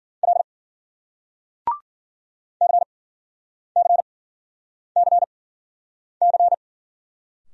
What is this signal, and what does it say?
Morse code, heard as a keyed tone, about 54 wpm, spaced as for 9 wpm, 700 Hz. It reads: L CCCC